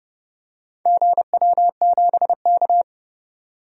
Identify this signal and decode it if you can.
Morse code, heard as a keyed tone, about 32 wpm, 700 Hz. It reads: GW7K
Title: DXCC-100 Call Signs - 4 Characters - 30wpm